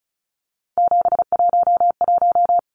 Morse code, heard as a keyed tone, about 34 wpm, 700 Hz. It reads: 711
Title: Numbers - 3-Digit 35wpm